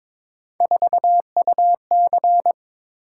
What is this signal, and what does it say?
Morse code, heard as a keyed tone, about 22 wpm, 700 Hz. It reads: 4UC
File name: practice / 22wpm / Three Letter-Number Combinations - 22wpm